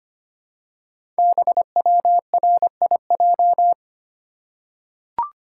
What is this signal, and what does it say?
Morse code, heard as a keyed tone, about 25 wpm, 690 Hz. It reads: BWRIJ
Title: Code Groups - 5 Letters - 25wpm